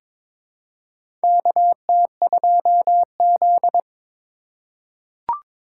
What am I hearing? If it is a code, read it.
Morse code, heard as a keyed tone, about 22 wpm, 700 Hz. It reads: KT2Z
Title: USA Call Signs - 4 Characters - 4x Character Spacing - 22wpm